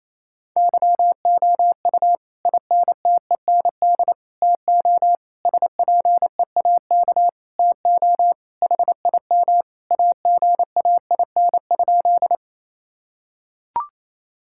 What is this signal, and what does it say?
Morse code, heard as a keyed tone, about 28 wpm, 700 Hz. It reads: YOU INTEND TO SPEAK TO HIM AGAIN?
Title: Sentences from Top 1600 Words 28wpm